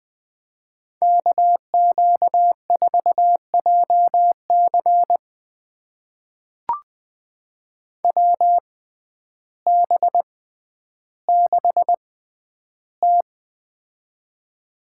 Morse code, heard as a keyed tone, about 20 wpm, spaced as for 11 wpm, 700 Hz. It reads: KQ4JC WB6T
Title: USA Call Signs - 6x Character Spacing - 20wpm